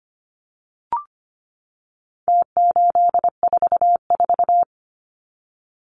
Morse code, heard as a keyed tone, about 25 wpm, 700 Hz. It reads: T844
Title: Numbers - Contest Style - 4-Digit 25wpm